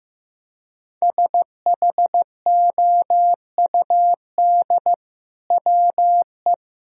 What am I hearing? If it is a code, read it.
Morse code, heard as a keyed tone, about 16 wpm, 700 Hz. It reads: SHOUD WE
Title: Sentences from Top 1000 Words - Review Part 4 - 15wpm